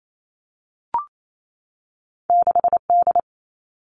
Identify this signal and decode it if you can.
Morse code, heard as a keyed tone, about 28 wpm, 700 Hz. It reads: 6D
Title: All Cut Numbers - 2-Digit 28wpm